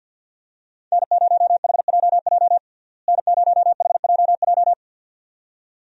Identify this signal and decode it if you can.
Morse code, heard as a keyed tone, about 50 wpm, 700 Hz. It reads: N0HJJ N0HJJ
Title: USA Call Signs - 5 Characters - Repeated 2x - 50wpm